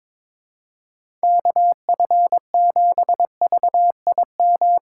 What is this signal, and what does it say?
Morse code, heard as a keyed tone, about 22 wpm, 700 Hz. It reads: KF7VIM
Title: USA Call Signs - 6 Characters - 4x Character Spacing - 22wpm